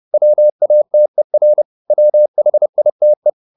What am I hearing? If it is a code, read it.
Morse code, heard as a keyed tone, about 30 wpm, 600 Hz. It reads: WATER WHITE
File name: the old man and the sea / the old man and the sea 1 30wpm